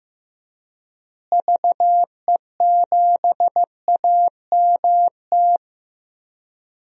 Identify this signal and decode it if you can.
Morse code, heard as a keyed tone, about 16 wpm, 700 Hz. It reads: VE7AMT
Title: DXCC-100 Call Signs - 6 Characters - Repeated 2x - 15wpm